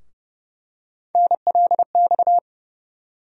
Morse code, heard as a keyed tone, about 30 wpm, 700 Hz. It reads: NLX
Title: USA Call Sign Postfixes - 3 Letters 30wpm